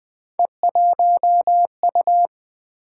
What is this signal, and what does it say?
Morse code, heard as a keyed tone, about 20 wpm, 700 Hz. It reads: E1U
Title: Three Letter-Number Combinations - 20wpm